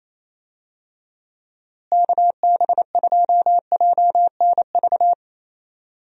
Morse code, heard as a keyed tone, about 28 wpm, 700 Hz. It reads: KB2JNV
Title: USA Call Signs - 6 Characters - Repeated 3x - 28wpm